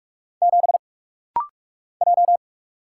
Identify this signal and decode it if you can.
Morse code, heard as a keyed tone, about 44 wpm, 700 Hz. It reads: 7 J
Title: Single Letter-Number Alt - Rapid-Fire 45wpm